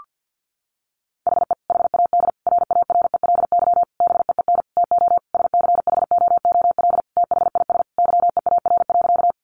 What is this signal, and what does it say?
Morse code, heard as a keyed tone, about 50 wpm, 700 Hz. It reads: HE HAD RARELY BEEN TO SCHOOL THIS YEAR.
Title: Sentences from Top 1400 Words 50wpm